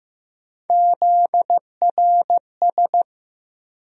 Morse code, heard as a keyed tone, about 15 wpm, 700 Hz. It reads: ZRS
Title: USA Call Sign Postfixes - 3 Letters 15wpm